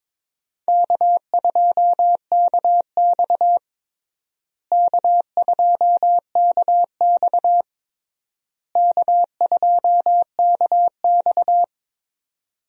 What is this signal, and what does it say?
Morse code, heard as a keyed tone, about 22 wpm, 700 Hz. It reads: K2KX K2KX K2KX
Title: USA Call Signs - 4 Characters - Repeated 3x - 22wpm